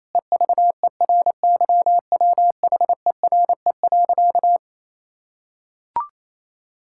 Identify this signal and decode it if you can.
Morse code, heard as a keyed tone, about 28 wpm, 700 Hz. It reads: EVERYWHERE.
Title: Sentences from Top 1600 Words 28wpm